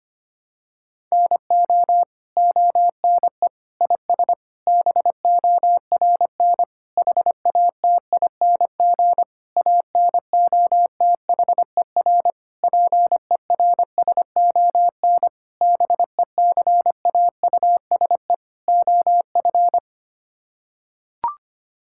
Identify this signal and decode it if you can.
Morse code, heard as a keyed tone, about 25 wpm, 700 Hz. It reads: NO ONE IS BORN HATING ANOTHER PERSON BECAUSE OF
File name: Sets of 9 Words - 25wpm